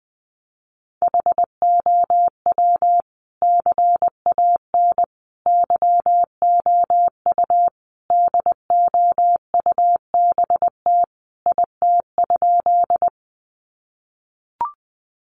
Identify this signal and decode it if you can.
Morse code, heard as a keyed tone, about 20 wpm, 700 Hz. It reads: HOW CAN YOU DOUBT IT?
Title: Sentences from Top 1500 Words 20wpm